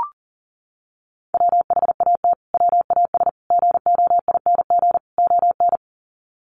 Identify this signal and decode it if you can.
Morse code, heard as a keyed tone, about 40 wpm, 700 Hz. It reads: WHAT WAS GOING ON
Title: Sets of 4 Words 40wpm